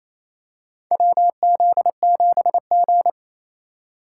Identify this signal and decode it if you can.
Morse code, heard as a keyed tone, about 28 wpm, 700 Hz. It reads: WZ7G
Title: USA Call Signs - 3x Character Spacing - 28wpm